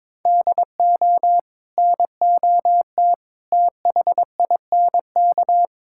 Morse code, heard as a keyed tone, about 22 wpm, 700 Hz. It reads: DO NOT THINK
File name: practice / 22wpm / Build A Phrase - 5-Word Phrases - 22wpm